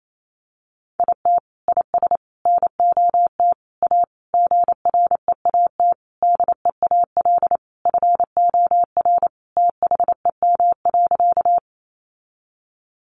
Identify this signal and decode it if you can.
Morse code, heard as a keyed tone, about 28 wpm, 700 Hz. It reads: IT IS NOT A GREAT DEAL FOR THEM.
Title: Sentences from Top 800 Words 28wpm